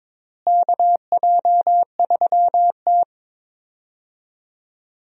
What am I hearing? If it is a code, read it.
Morse code, heard as a keyed tone, about 22 wpm, 700 Hz. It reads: KJ3T
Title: USA Call Signs - 4 Characters - Repeated 3x - 22wpm